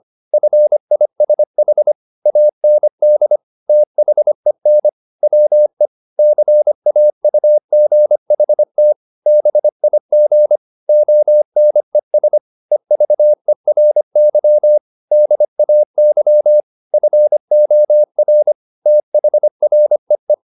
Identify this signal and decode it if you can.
Morse code, heard as a keyed tone, about 25 wpm, 600 Hz. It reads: FISH AND THEN WE CAUGHT BIG ONES EVERY DAY FOR THREE